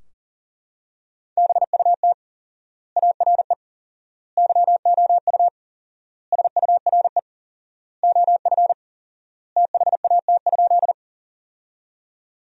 Morse code, heard as a keyed tone, about 37 wpm, 700 Hz. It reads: BUT ARE YOU SURE OF THAT?
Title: 4x Word-Spacing - Sentences from Top 300 Words - 40wpm